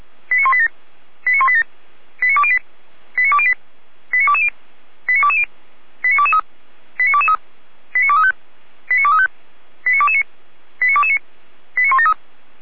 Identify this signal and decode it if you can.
Five-tone selective call (ZVEI), heard as quick runs of five tones, tones of about 70 ms. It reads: 89187 89187 89208 89208 89220 89220 89223 89223 89235 89235 89229 89229 89173
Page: /455/SDR/ZVEI/89240/